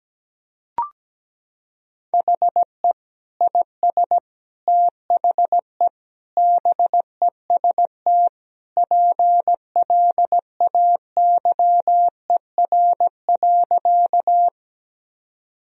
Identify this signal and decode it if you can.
Morse code, heard as a keyed tone, about 17 wpm, 700 Hz. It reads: HE IS THE BEST PLAYER.
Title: Sentences from Top 1000 Words - Review Part 4 - 17wpm